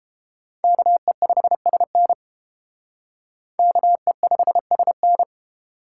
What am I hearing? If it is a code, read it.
Morse code, heard as a keyed tone, about 29 wpm, 700 Hz. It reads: KE5SN KE5SN